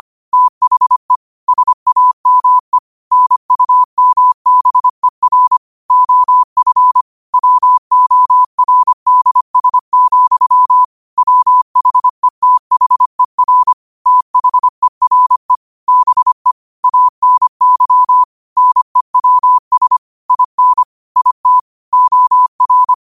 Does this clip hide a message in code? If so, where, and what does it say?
Yes: Morse code from 0.3 to 22.9 s, heard as a keyed tone, about 25 wpm, 1000 Hz. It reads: THE SAME NUMBER OF WORDS, WHETHER THERE BE ANY NEWS IN IT OR